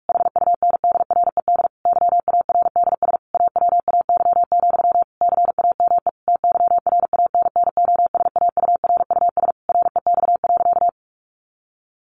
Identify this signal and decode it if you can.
Morse code, heard as a keyed tone, about 44 wpm, 700 Hz. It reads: HUNDRED YARDS AWAY, CAME TYRANNOSAURUS REX.
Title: A Sound Of Thunder 45wpm